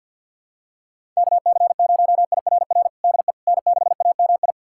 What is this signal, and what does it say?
Morse code, heard as a keyed tone, about 51 wpm, 700 Hz. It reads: KC0IRR DE N6AMI
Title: USA Callsign DE Callsign 50wpm